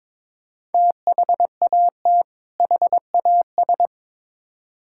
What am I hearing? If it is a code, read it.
Morse code, heard as a keyed tone, about 22 wpm, 700 Hz. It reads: THAT HAS
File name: Build A Phrase - 2-Word Phrases - 22wpm